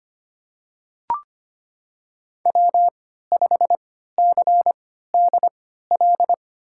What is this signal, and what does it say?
Morse code, heard as a keyed tone, about 25 wpm, spaced as for 14 wpm, 700 Hz. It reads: W5CDL
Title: USA Call Signs - 5 Characters - 3x Character Spacing - 25wpm